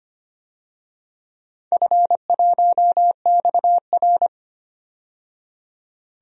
Morse code, heard as a keyed tone, about 25 wpm, 700 Hz. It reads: F1XR
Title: DXCC-100 Call Signs - 4 Characters - 25wpm